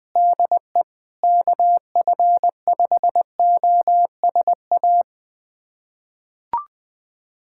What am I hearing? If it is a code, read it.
Morse code, heard as a keyed tone, about 20 wpm, 700 Hz. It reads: DE KF5OSA
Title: USA Callsign DE Callsign - 6 Characters - 20wpm